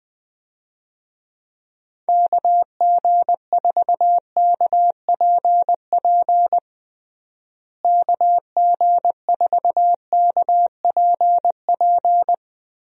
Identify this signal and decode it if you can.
Morse code, heard as a keyed tone, about 20 wpm, 700 Hz. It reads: KG4KPP KG4KPP
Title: USA Call Signs - 6 Characters - Repeated 3x - 20wpm